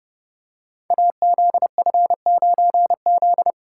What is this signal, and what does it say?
Morse code, heard as a keyed tone, about 30 wpm, 700 Hz. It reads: AZF9Z
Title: Code Groups - 5 Letter:Numbers - 30wpm